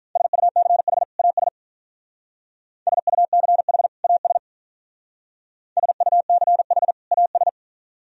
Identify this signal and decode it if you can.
Morse code, heard as a keyed tone, about 43 wpm, 700 Hz. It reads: SUCH AS SUCH AS SUCH AS
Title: Speed-Racing - Sets of 2 Words 35wpm